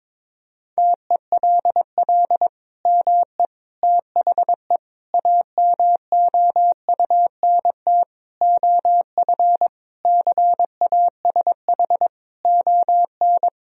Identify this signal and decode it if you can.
Morse code, heard as a keyed tone, about 21 wpm, 700 Hz. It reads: TELL ME THE AMOUNT OF CASH ON
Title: Sentences from Top 1000 Words - Review Part 4 - 22wpm